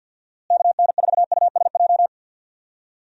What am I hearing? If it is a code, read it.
Morse code, heard as a keyed tone, about 52 wpm, 700 Hz. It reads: KN4USJ
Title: USA Call Signs - 6 Characters - Repeated 3x - 50wpm